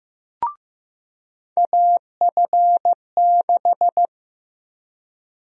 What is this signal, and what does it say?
Morse code, heard as a keyed tone, about 15 wpm, 700 Hz. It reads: AF6
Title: USA Call Sign Prefixes - 15wpm